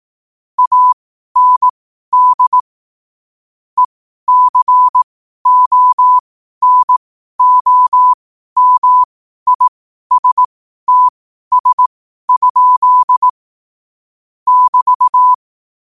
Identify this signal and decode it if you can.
Morse code, heard as a keyed tone, about 18 wpm, spaced as for 12 wpm, 1000 Hz. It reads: AND ECONOMISTS? <BT>
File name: qotdfiles / 12wpm